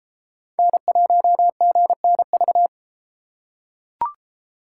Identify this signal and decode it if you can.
Morse code, heard as a keyed tone, about 33 wpm, 700 Hz. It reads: N1GNV